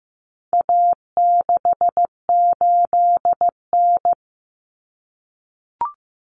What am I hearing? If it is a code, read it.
Morse code, heard as a keyed tone, about 15 wpm, 700 Hz. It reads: A68N
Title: Common Cut Numbers - 4-Digit 15wpm